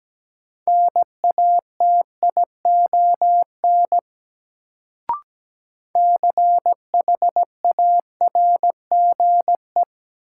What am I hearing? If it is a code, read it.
Morse code, heard as a keyed tone, about 17 wpm, 700 Hz. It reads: NATION CHARGE
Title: Sentences from Top 900 Words - New Word Drill - 17wpm